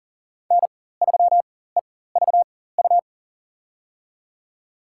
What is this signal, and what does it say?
Morse code, heard as a keyed tone, about 40 wpm, spaced as for 19 wpm, 700 Hz. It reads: N3EVU
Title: USA Call Signs - 4x Character Spacing - 40wpm